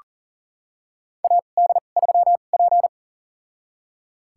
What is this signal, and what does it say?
Morse code, heard as a keyed tone, about 37 wpm, spaced as for 28 wpm, 700 Hz. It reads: AD3P
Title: USA Call Signs - 4 Characters - 2x Character Spacing - 40wpm